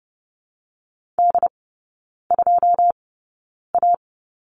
Morse code, heard as a keyed tone, about 30 wpm, spaced as for 9 wpm, 700 Hz. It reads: D2A
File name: Lesson 17e - Letter u - Three Characters 30wpm